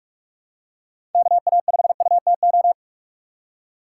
Morse code, heard as a keyed tone, about 42 wpm, 700 Hz. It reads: KA5UTO
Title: USA Call Signs - 6 Characters - 3x Character Spacing - 45wpm